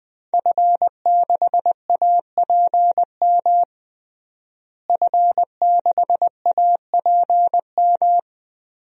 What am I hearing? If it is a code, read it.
Morse code, heard as a keyed tone, about 20 wpm, 700 Hz. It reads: F6APM F6APM
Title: DXCC-100 Call Signs - 5 Characters - Repeated 3x - 20wpm